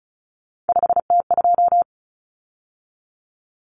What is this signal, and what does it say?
Morse code, heard as a keyed tone, about 34 wpm, 700 Hz. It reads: 5T2
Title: Common Cut Numbers - 3-Digit 35wpm